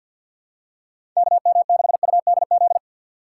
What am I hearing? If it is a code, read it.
Morse code, heard as a keyed tone, about 48 wpm, 700 Hz. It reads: KM6UDZ
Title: USA Call Signs - 3x Character Spacing - 50wpm